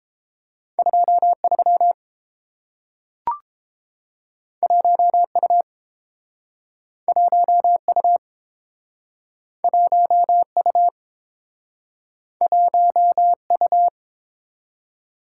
Morse code, heard as a keyed tone, about 32 wpm, 700 Hz. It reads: 23 1U 1U 1U 1U